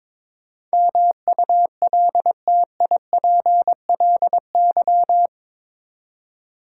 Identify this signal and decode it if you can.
Morse code, heard as a keyed tone, about 21 wpm, 700 Hz. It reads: MULTIPLY
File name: practice / 22wpm / Eight Letter Words - 22wpm